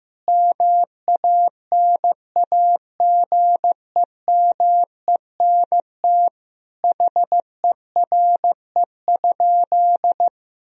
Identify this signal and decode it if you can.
Morse code, heard as a keyed tone, about 15 wpm, 700 Hz. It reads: MANAGEMENT HERE?
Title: Sentences from Top 1000 Words - Review Part 4 - 15wpm